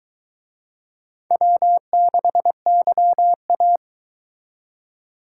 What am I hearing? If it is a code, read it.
Morse code, heard as a keyed tone, about 23 wpm, 700 Hz. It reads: W6YA